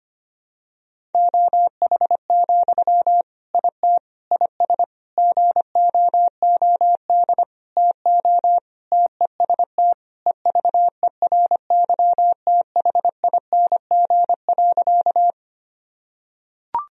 Morse code, heard as a keyed tone, about 25 wpm, 700 Hz. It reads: OH, IT IS GOOD TO TEST EVERYTHING.
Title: Sentences from Top 1000 Words - Review Part 3 - 25wpm